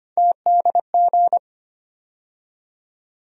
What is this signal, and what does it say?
Morse code, heard as a keyed tone, about 25 wpm, 700 Hz. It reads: TDG
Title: USA Call Sign Postfixes - 3 Letters 25wpm